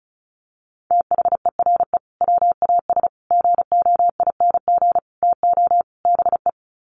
Morse code, heard as a keyed tone, about 34 wpm, 700 Hz. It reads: THERE WAS GOING TO BE
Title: Sets of 5 Words Encore2 35wpm